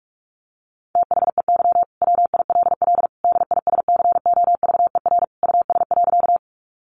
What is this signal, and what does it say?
Morse code, heard as a keyed tone, about 44 wpm, 700 Hz. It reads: THEY WILL DISCOVER US.
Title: Sentences from Top 1100 Words 45wpm